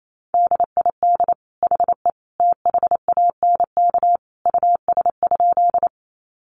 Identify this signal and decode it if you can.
Morse code, heard as a keyed tone, about 28 wpm, 700 Hz. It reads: DID HE THANK US?
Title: Sentences from Top 600 Words 28wpm